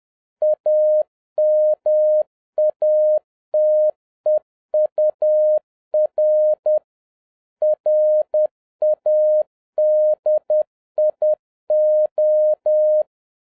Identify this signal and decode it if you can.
Morse code, heard as a keyed tone, about 10 wpm, 610 Hz. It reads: AMATEUR RADIO